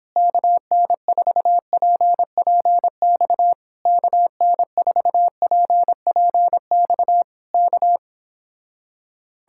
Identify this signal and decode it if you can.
Morse code, heard as a keyed tone, about 26 wpm, 700 Hz. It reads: KN4PPX KN4PPX K